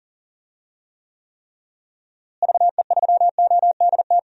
Morse code, heard as a keyed tone, about 40 wpm, 700 Hz. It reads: VE3ODT
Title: DXCC-100 Call Signs - 6 Characters - Repeated 3x - 40wpm